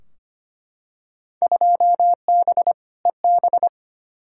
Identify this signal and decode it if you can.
Morse code, heard as a keyed tone, about 25 wpm, 700 Hz. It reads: 2B EB
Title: Field Day Exchanges - 25wpm